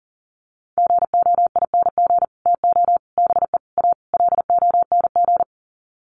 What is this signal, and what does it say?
Morse code, heard as a keyed tone, about 40 wpm, 700 Hz. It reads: GOING TO BE A LONG
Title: Sets of 5 Words Encore2 40wpm